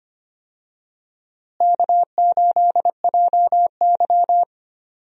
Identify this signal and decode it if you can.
Morse code, heard as a keyed tone, about 25 wpm, 700 Hz. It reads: K8JY
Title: USA Call Signs - 4 Characters - Repeated 2x - 25wpm